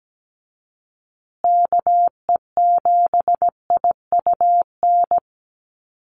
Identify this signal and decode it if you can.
Morse code, heard as a keyed tone, about 17 wpm, 700 Hz. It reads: KE7IUN
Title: USA Call Signs Repeated 3x 17wpm